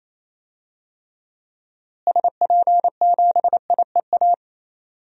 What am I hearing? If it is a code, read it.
Morse code, heard as a keyed tone, about 28 wpm, 700 Hz. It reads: SP7IEA
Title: DXCC-100 Call Signs - 6 Characters - Repeated 3x - 28wpm